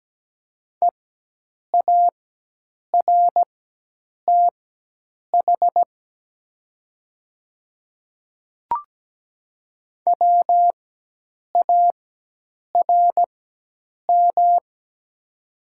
Morse code, heard as a keyed tone, about 17 wpm, spaced as for 8 wpm, 700 Hz. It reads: EARTH WARM
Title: 4x Character Spacing - Top 500 Words - 17wpm